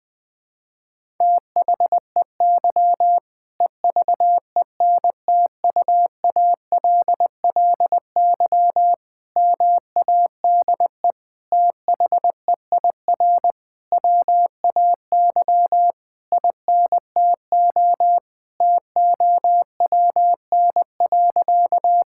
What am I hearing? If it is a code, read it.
Morse code, heard as a keyed tone, about 20 wpm, 700 Hz. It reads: THEY EVENTUALLY MADE THEIR WAY INTO TOWN.
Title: Sentences from Top 1000 Words - Review Part 6 - 20wpm